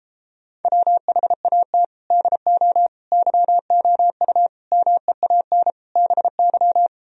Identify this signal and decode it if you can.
Morse code, heard as a keyed tone, about 33 wpm, 700 Hz. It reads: WHAT DO YOU MEAN BY